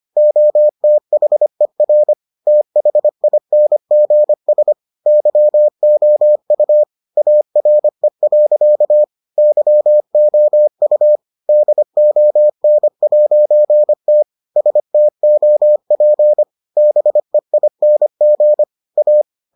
Morse code, heard as a keyed tone, about 25 wpm, 600 Hz. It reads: OTHER THINGS YOU ARE. YOU DON'T STOP BEING A